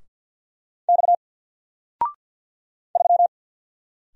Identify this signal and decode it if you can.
Morse code, heard as a keyed tone, about 50 wpm, 700 Hz. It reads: X 3
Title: Single Letter-Number Alt 50wpm